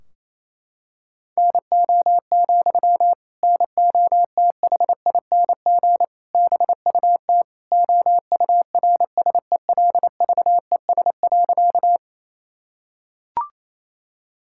Morse code, heard as a keyed tone, about 28 wpm, 700 Hz. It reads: NO, NOTHING BUT OURSELVES.
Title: Sentences from Top 1600 Words 28wpm